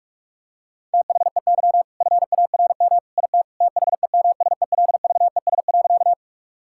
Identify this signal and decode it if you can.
Morse code, heard as a keyed tone, about 46 wpm, 700 Hz. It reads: THEY FARM IT THEMSELVES.
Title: Sentences from Top 1000 Words - Review Part 5 - 45wpm